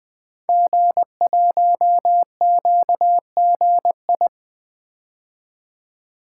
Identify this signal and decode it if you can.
Morse code, heard as a keyed tone, about 20 wpm, 700 Hz. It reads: G1QGI
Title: Code Groups - 5 Letter:Numbers - 20wpm